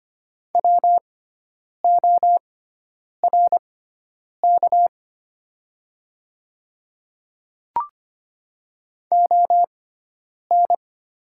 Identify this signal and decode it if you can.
Morse code, heard as a keyed tone, about 25 wpm, spaced as for 9 wpm, 700 Hz. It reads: WORK ON
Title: 6x Character Spacing - Top 300 Words - 25wpm